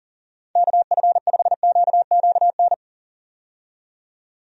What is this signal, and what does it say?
Morse code, heard as a keyed tone, about 38 wpm, 700 Hz. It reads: KF5QQN
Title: USA Call Signs - 6 Characters - 40wpm